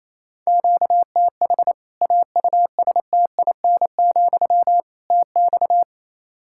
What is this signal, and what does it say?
Morse code, heard as a keyed tone, about 28 wpm, 700 Hz. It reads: QTH AUSTIN, TX
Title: Practice QSO - Exchange 1 - Beginning - 28wpm